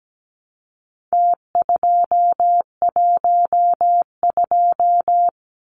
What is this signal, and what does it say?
Morse code, heard as a keyed tone, about 17 wpm, 700 Hz. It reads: T212
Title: Numbers - Contest Style - 4-Digit 17wpm